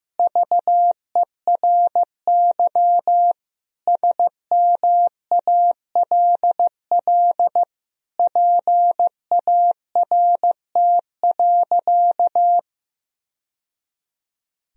Morse code, heard as a keyed tone, about 15 wpm, 700 Hz. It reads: VERY SMALL PART.
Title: Sentences from Top 1000 Words - Review Part 1 - 15wpm